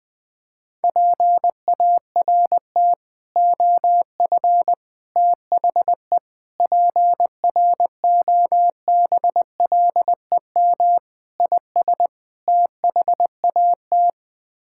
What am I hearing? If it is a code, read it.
Morse code, heard as a keyed tone, about 21 wpm, 700 Hz. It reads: PART OF THE PROBLEM IS THAT
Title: Sets of 6 Words - 20wpm